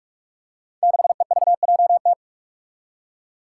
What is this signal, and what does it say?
Morse code, heard as a keyed tone, about 44 wpm, 700 Hz. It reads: 6EVJT
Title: Five Challenging Letter-Number Combinations 45wpm